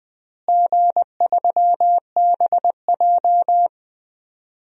Morse code, heard as a keyed tone, about 20 wpm, 700 Hz. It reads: G3BJ
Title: DXCC-100 Call Signs - 4 Characters - Repeated 2x - 20wpm